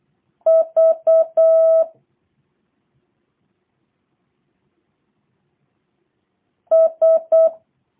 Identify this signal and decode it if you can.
Morse code, heard as a keyed tone, about 8 wpm, 660 Hz. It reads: V S